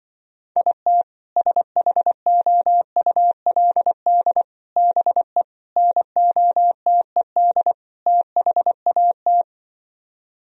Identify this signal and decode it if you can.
Morse code, heard as a keyed tone, about 25 wpm, 700 Hz. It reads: IT SHOULD BE NOTED THAT
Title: Speed-Racing - Sets of 5 Words 20wpm